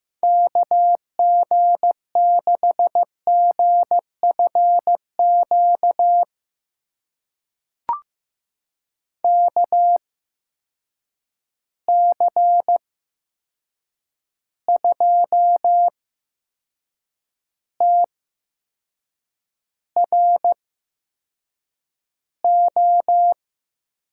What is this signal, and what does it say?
Morse code, heard as a keyed tone, about 15 wpm, spaced as for 6 wpm, 700 Hz. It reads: KG6GFQ KC2TRO
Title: USA Call Signs - 6 Characters - 8x Character Spacing - 15wpm